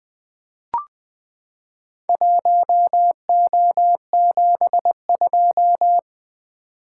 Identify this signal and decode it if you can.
Morse code, heard as a keyed tone, about 20 wpm, 700 Hz. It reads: 1O72
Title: Four Letter-Number Combinations - 20wpm